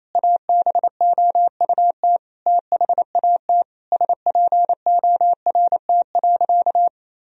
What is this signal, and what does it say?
Morse code, heard as a keyed tone, about 28 wpm, 700 Hz. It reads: ABOUT THAT SPORT.
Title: Sentences from Top 1000 Words - Review Part 4 - 28wpm